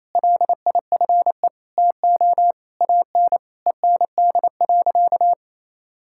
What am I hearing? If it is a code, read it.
Morse code, heard as a keyed tone, about 27 wpm, 700 Hz. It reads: LIFE TO AN END.
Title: Sentences from Top 1000 Words - Review Part 5 - 28wpm